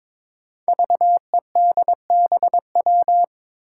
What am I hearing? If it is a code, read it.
Morse code, heard as a keyed tone, about 22 wpm, 700 Hz. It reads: VEDBW
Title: Five Challenging Letter-Number Combinations 22wpm